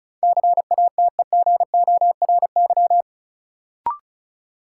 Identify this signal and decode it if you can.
Morse code, heard as a keyed tone, about 34 wpm, 700 Hz. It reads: CATEGORY
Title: Sentences from Top 1000 Words - Review Part 7 - New Word Drill - 35wpm